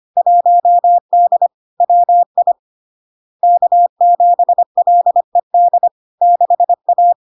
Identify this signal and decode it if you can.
Morse code, heard as a keyed tone, about 25 wpm, 700 Hz. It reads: 1D WI K7LED 6A
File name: LARK-FD-2022-Log-in-CW-25wpm-700-Hz